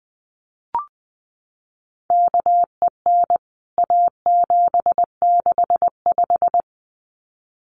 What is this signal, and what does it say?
Morse code, heard as a keyed tone, about 20 wpm, 700 Hz. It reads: KEN A765
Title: CWT Exchange 20wpm